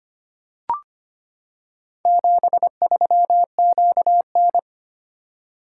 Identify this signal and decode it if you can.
Morse code, heard as a keyed tone, about 25 wpm, 700 Hz. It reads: 73QN
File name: practice / 25wpm / Four Letter-Number Combinations - 25wpm